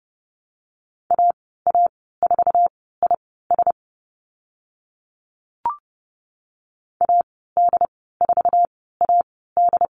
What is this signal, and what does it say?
Morse code, heard as a keyed tone, about 30 wpm, spaced as for 17 wpm, 690 Hz. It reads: AA4IS AD4AD
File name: Lesson 15g - Number 5 - Call Signs 30wpm